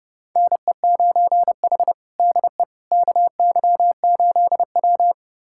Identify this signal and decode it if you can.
Morse code, heard as a keyed tone, about 30 wpm, 700 Hz. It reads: NE9H DE KY8W
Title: USA Callsign DE Callsign - 4 Characters - 30wpm